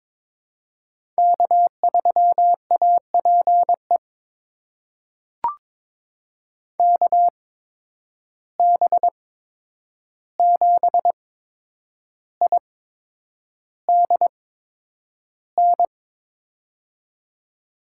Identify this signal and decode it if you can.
Morse code, heard as a keyed tone, about 22 wpm, spaced as for 9 wpm, 700 Hz. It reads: K3APE KB7IDN
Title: USA Call Signs - 8x Character Spacing - 22wpm